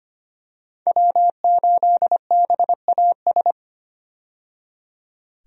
Morse code, heard as a keyed tone, about 25 wpm, 700 Hz. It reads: W8BAS
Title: USA Call Signs - 5 Characters - 25wpm